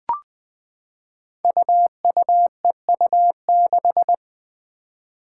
Morse code, heard as a keyed tone, about 20 wpm, 700 Hz. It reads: UUEU6
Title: Five Challenging Letter-Number Combinations 20wpm